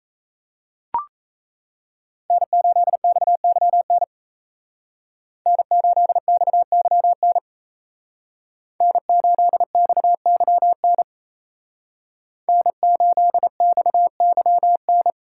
Morse code, heard as a keyed tone, about 34 wpm, 700 Hz. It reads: N8XYN N8XYN N8XYN N8XYN
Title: Speed Racing - USA Call Signs 28wpm